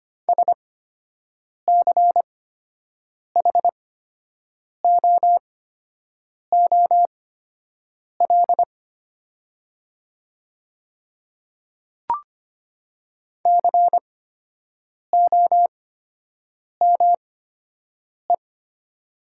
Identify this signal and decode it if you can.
Morse code, heard as a keyed tone, about 25 wpm, spaced as for 7 wpm, 700 Hz. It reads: SCHOOL COME
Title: 8x Character Spacing - Top 500 Words - 25wpm